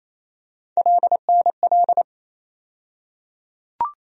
Morse code, heard as a keyed tone, about 27 wpm, 700 Hz. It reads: LNL
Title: USA Call Sign Postfixes - 3 Letters 28wpm